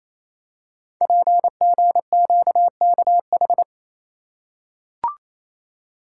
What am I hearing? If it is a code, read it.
Morse code, heard as a keyed tone, about 29 wpm, 690 Hz. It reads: PGQKH
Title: Code Groups - 5 Letters - 28wpm